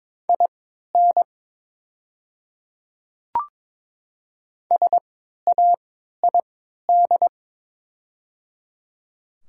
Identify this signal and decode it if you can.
Morse code, heard as a keyed tone, about 23 wpm, spaced as for 13 wpm, 700 Hz. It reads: IN SAID
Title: 3x Character Spacing - Top 100 Words - 22wpm